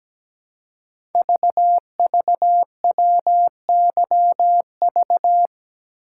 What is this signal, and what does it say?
Morse code, heard as a keyed tone, about 17 wpm, 700 Hz. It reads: VVWYV
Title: Code Groups - 5 Letters - 17wpm